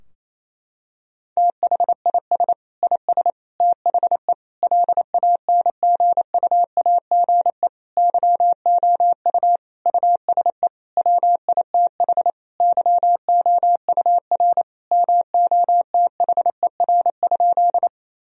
Morse code, heard as a keyed tone, about 28 wpm, 700 Hz. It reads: THIS IS THE LANGUAGE YOU USE WITH YOUR MOTHER?
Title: Sentences from Top 1000 Words - Review Part 3 - 28wpm